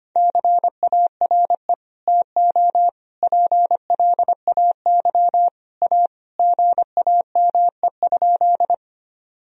Morse code, heard as a keyed tone, about 25 wpm, 700 Hz. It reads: CARE TO PLAY A GAME?
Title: Sentences from Top 1000 Words - Review Part 1 - 25wpm